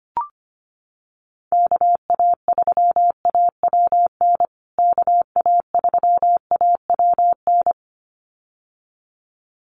Morse code, heard as a keyed tone, about 25 wpm, 700 Hz. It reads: KA3AWN KA3AWN
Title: USA Call Signs Repeated 2x 25wpm